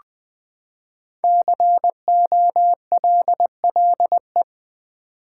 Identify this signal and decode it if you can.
Morse code, heard as a keyed tone, about 20 wpm, 700 Hz. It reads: COLLE
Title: Build A Word - 5 Letters At A Time - 10-Letter Words - 20wpm